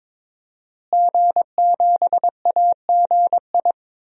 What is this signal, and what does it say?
Morse code, heard as a keyed tone, about 22 wpm, 700 Hz. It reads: G7AGI
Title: DXCC-100 Call Signs - 5 Characters - Repeated 3x - 22wpm